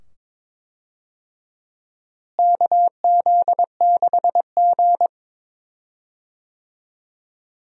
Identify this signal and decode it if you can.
Morse code, heard as a keyed tone, about 22 wpm, 700 Hz. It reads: KZ6G
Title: USA Call Signs - 4 Characters - Repeated 2x - 22wpm